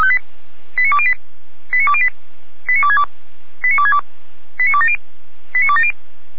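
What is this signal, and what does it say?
Five-tone selective call (ZVEI), heard as quick runs of five tones, tones of about 70 ms. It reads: 89208 89208 89262 89262 89270 89270
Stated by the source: ZVEI five-tone call 89340, received by SDR.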